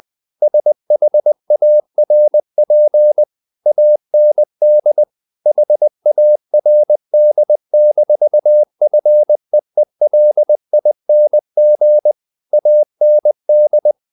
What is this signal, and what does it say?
Morse code, heard as a keyed tone, about 20 wpm, 600 Hz. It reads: SHARP AND HARD-FEELING AND
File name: the old man and the sea part 2 20wpm